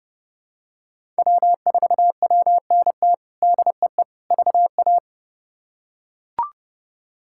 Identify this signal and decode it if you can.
Morse code, heard as a keyed tone, about 30 wpm, 700 Hz. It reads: W4WNT DEE VA
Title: K1USN SST QSO Practice - 30wpm